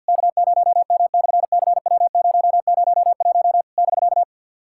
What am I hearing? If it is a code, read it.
Morse code, heard as a keyed tone, about 50 wpm, 700 Hz. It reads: K0M/CW001 <BK>
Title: SOTA QSO Practice 50wpm